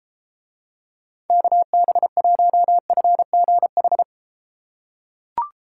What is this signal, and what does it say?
Morse code, heard as a keyed tone, about 31 wpm, 700 Hz. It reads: KB1FGH